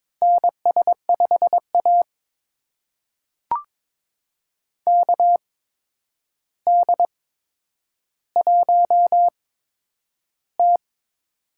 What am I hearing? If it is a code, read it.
Morse code, heard as a keyed tone, about 22 wpm, spaced as for 9 wpm, 700 Hz. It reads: NS5A KD1T
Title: USA Call Signs - 4 Characters - 8x Character Spacing - 22wpm